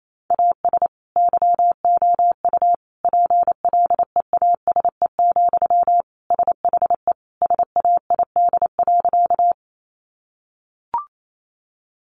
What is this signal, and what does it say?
Morse code, heard as a keyed tone, about 28 wpm, 700 Hz. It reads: AS YOU PLEASE, SHE SAID.
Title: Sentences from Top 1500 Words 28wpm